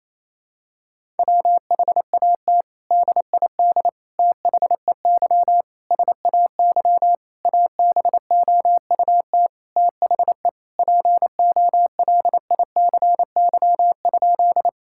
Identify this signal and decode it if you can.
Morse code, heard as a keyed tone, about 28 wpm, 700 Hz. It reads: WHAT DID THEY SAY ABOUT THE POLICY?
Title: Sentences from Top 1000 Words - Review Part 5 - 28wpm